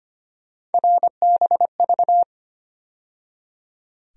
Morse code, heard as a keyed tone, about 25 wpm, 700 Hz. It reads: RBV
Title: USA Call Sign Postfixes - 3 Letters 25wpm